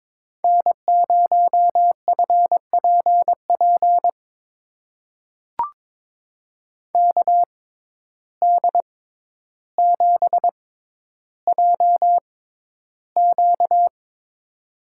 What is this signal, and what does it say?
Morse code, heard as a keyed tone, about 22 wpm, spaced as for 11 wpm, 700 Hz. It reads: N0FPP KD7JQ
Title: USA Call Signs - 5 Characters - 6x Character Spacing - 22wpm